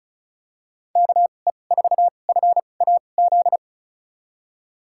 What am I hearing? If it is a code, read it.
Morse code, heard as a keyed tone, about 34 wpm, spaced as for 25 wpm, 700 Hz. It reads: KE4FAZ
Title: USA Call Signs - 2x Character Spacing - 35wpm